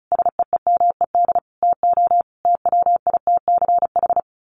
Morse code, heard as a keyed tone, about 34 wpm, 700 Hz. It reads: SEEMED TO TWITCH
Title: A Sound Of Thunder 35wpm